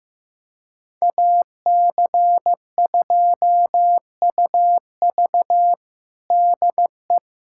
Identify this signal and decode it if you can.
Morse code, heard as a keyed tone, about 16 wpm, 700 Hz. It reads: AC2UV DE
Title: USA Callsign DE Callsign - 5 Characters - 15wpm